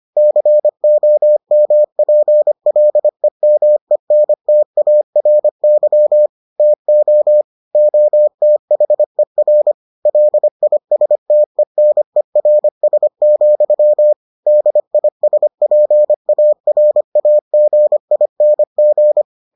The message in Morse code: COMPLEMENTARY TO OTHER LISTENERS, DISPARAGING

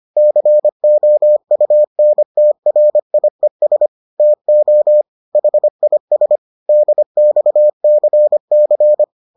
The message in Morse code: COUNTRIES TO HIS DXCC

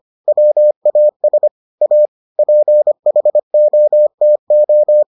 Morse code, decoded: WAS A PHOTO

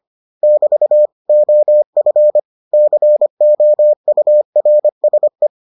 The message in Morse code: <BT> OF COURSE

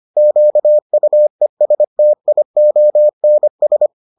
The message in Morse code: QUESTIONS

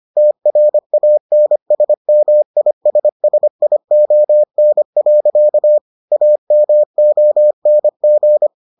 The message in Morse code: TRANSMISSION. AMONG

